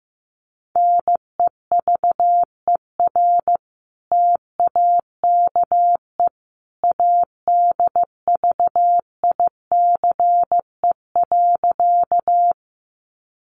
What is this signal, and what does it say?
Morse code, heard as a keyed tone, about 15 wpm, 700 Hz. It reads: NEVER TAKE ADVICE.
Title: Sentences from Top 1100 Words 15wpm